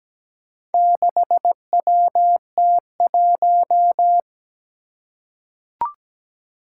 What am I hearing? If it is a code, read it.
Morse code, heard as a keyed tone, about 17 wpm, 700 Hz. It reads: 6WT1
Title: Four Challenging Letter-Number Combinations 17wpm